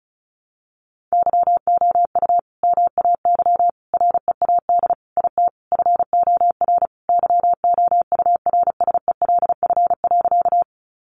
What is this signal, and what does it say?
Morse code, heard as a keyed tone, about 34 wpm, 700 Hz. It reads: YOU MAY READ IT FOR YOURSELF.
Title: Sentences from Top 600 Words 35wpm